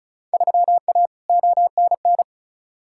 Morse code, heard as a keyed tone, about 36 wpm, 690 Hz. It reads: 3A ONN
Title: Field Day Exchanges Encore - 35wpm